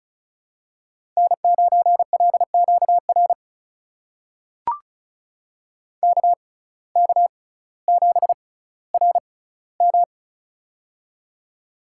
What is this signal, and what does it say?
Morse code, heard as a keyed tone, about 36 wpm, spaced as for 18 wpm, 700 Hz. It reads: N9LQR KK7RM
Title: USA Call Signs - 5 Characters - 6x Character Spacing - 35wpm